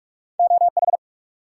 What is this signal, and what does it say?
Morse code, heard as a keyed tone, about 42 wpm, 700 Hz. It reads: OH
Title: Sentences from Top 500 Words - New Word Drill - 45wpm